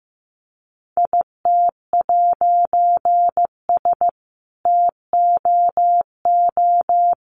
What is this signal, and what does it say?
Morse code, heard as a keyed tone, about 15 wpm, 700 Hz. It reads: IT'S TOO
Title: A Sound Of Thunder 15wpm